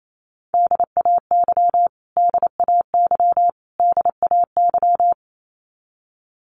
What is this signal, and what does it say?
Morse code, heard as a keyed tone, about 28 wpm, 700 Hz. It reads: DAY DAY DAY
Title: Top 500 Words Repeated 28wpm